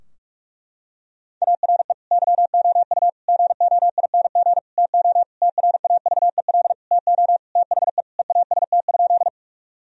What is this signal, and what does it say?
Morse code, heard as a keyed tone, about 46 wpm, 700 Hz. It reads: ARE YOU GOING TO TRAVEL TO THE EAST?
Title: Sentences from Top 1000 Words - Review Part 3 - 45wpm